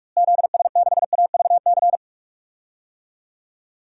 Morse code, heard as a keyed tone, about 41 wpm, 700 Hz. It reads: ZS6AVC
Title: DXCC-100 Call Signs - 6 Characters - Repeated 3x - 45wpm